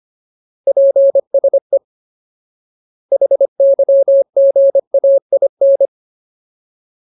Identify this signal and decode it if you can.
Morse code, heard as a keyed tone, about 25 wpm, 550 Hz. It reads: PSE HYGAIN
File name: Ultimate Word List / 25WPM Ultimate